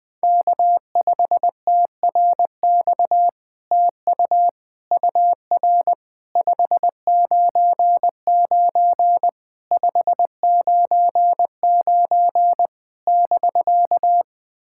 Morse code, heard as a keyed tone, about 20 wpm, 700 Hz. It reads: K5TRX TU UR 599 599 <BK>
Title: SOTA QSO Practice 20wpm